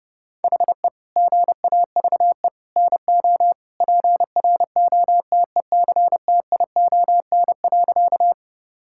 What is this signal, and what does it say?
Morse code, heard as a keyed tone, about 30 wpm, 700 Hz. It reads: HE GAVE NO PROTECTION.
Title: Sentences from Top 1600 Words 30wpm